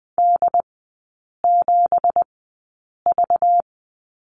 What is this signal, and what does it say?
Morse code, heard as a keyed tone, about 20 wpm, spaced as for 8 wpm, 700 Hz. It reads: D7V
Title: 212 Lesson 34e - Number 8 - Three Characters 20wpm